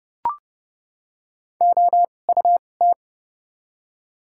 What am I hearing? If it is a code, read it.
Morse code, heard as a keyed tone, about 30 wpm, spaced as for 22 wpm, 700 Hz. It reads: OUT
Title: 2x Character Spacing - Top 100 Words - 30wpm